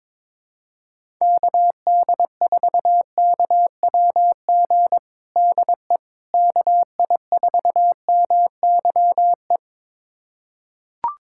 Morse code, heard as a keyed tone, about 21 wpm, 700 Hz. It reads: KD4KWG DE KI4MYE
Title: USA Callsign DE Callsign 22wpm